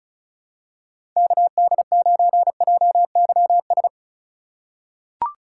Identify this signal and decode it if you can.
Morse code, heard as a keyed tone, about 36 wpm, 700 Hz. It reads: KD9JYS
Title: USA Call Signs - 6 Characters - 4x Character Spacing - 35wpm